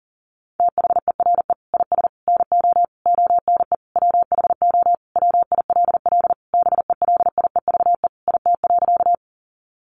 Morse code, heard as a keyed tone, about 40 wpm, 700 Hz. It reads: THERE IS NO ONE WHO WILL BELIEVE IT.
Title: Sentences from Top 1000 Words 40wpm